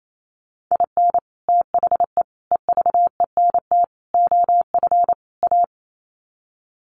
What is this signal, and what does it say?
Morse code, heard as a keyed tone, about 28 wpm, 700 Hz. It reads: IN THE EVENT OF A
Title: Sets of 5 Words Encore 28wpm